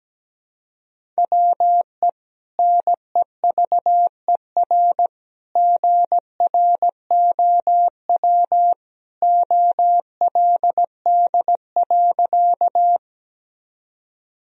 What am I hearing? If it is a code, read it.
Morse code, heard as a keyed tone, about 17 wpm, 700 Hz. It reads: WE NEVER GROW OLD.
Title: Sentences from Top 1000 Words - Review Part 1 - 17wpm